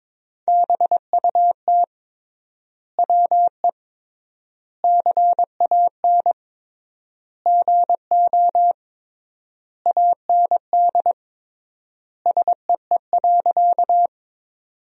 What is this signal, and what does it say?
Morse code, heard as a keyed tone, about 22 wpm, 700 Hz. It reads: BUT WE CAN GO AND SEE.
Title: 3x Word-Spacing - Sentences from Top 100 Words - 22wpm